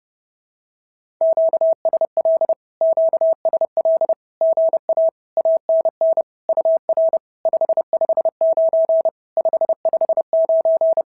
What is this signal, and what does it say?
Morse code, heard as a keyed tone, about 30 wpm, 650 Hz. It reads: QSL QSL GA ANN UR 559 559
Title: SOTA QSO Practice 30wpm